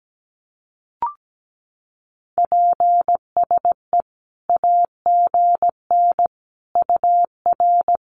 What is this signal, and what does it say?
Morse code, heard as a keyed tone, about 17 wpm, 700 Hz. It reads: PSE AGN UR
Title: Common QSO Phrases 17wpm